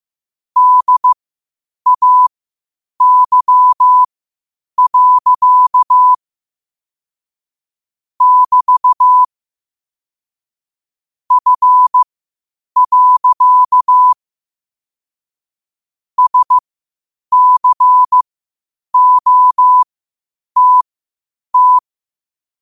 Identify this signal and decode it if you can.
Morse code, heard as a keyed tone, about 15 wpm, spaced as for 8 wpm, 1000 Hz. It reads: DAY. <BT> F. SCOTT
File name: qotdfiles / 8wpm